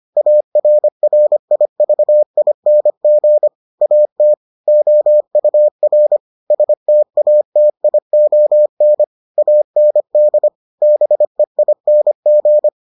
Morse code, heard as a keyed tone, about 25 wpm, 600 Hz. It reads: ARRIVING AT OUR STATION AND BEING